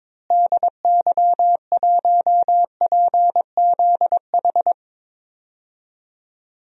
Morse code, heard as a keyed tone, about 22 wpm, 700 Hz. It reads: DY1PZH
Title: DXCC-100 Call Signs - 6 Characters - Repeated 3x - 22wpm